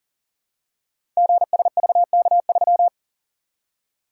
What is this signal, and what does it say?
Morse code, heard as a keyed tone, about 39 wpm, 700 Hz. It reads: GSVK3
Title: Code Groups - 5 Letter:Numbers - 40wpm